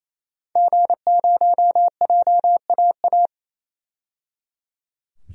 Morse code, heard as a keyed tone, about 28 wpm, 700 Hz. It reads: G0JAA
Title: DXCC-100 Call Signs - 5 Characters - 28wpm